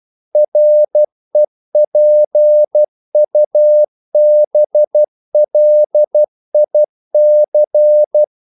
Morse code, heard as a keyed tone, about 12 wpm, 600 Hz. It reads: REPUBLIC
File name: CW / Practice / NationalCapitals12